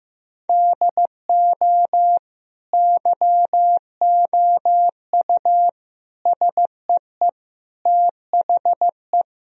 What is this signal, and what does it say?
Morse code, heard as a keyed tone, about 15 wpm, 700 Hz. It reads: DO YOU SEE THE
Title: Sentences from Top 1000 Words - Review Part 2 - 15wpm